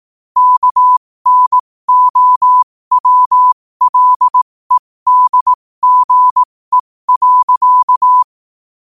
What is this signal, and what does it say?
Morse code, heard as a keyed tone, about 18 wpm, spaced as for 15 wpm, 1000 Hz. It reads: KNOWLEDGE.
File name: qotdfiles / 15wpm